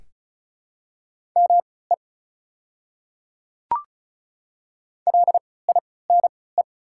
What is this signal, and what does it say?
Morse code, heard as a keyed tone, about 34 wpm, spaced as for 20 wpm, 700 Hz. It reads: ME LINE
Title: 3x Character Spacing - Top 200 Words - 35wpm